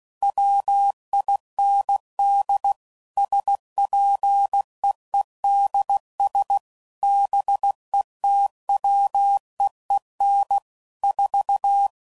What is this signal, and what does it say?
Morse code, heard as a keyed tone, about 16 wpm, 800 Hz. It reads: WIND SPEEDS BETWEEN 4
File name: CW Code Groups @15WPM/Weather Sets